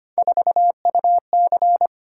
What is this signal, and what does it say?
Morse code, heard as a keyed tone, about 25 wpm, 700 Hz. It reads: 4UC
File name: Three Letter-Number Combinations - 25wpm